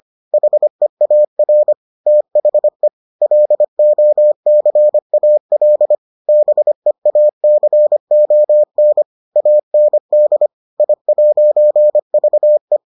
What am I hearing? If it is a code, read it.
Morse code, heard as a keyed tone, about 25 wpm, 600 Hz. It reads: HEAR THE LOCAL BEACON AND I'VE